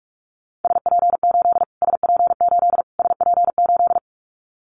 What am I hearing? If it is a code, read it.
Morse code, heard as a keyed tone, about 44 wpm, 700 Hz. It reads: SP8 SP8 SP8
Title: Callsign Prefixes for Top 25 DXCC - Repeated 3x - 45wpm